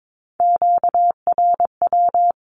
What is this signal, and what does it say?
Morse code, heard as a keyed tone, about 22 wpm, 700 Hz. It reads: QRW
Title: Q-Codes 22wpm